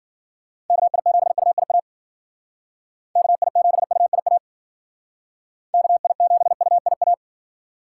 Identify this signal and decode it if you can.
Morse code, heard as a keyed tone, about 52 wpm, 700 Hz. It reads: KI7UIA KI7UIA KI7UIA